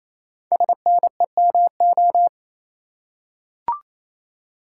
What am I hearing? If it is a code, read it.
Morse code, heard as a keyed tone, about 27 wpm, 700 Hz. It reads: SNEMO
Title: Code Groups - 5 Letters - 28wpm